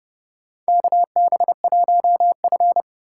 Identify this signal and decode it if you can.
Morse code, heard as a keyed tone, about 30 wpm, 700 Hz. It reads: KB1F
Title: USA Call Signs - 4 Characters - 4x Character Spacing - 30wpm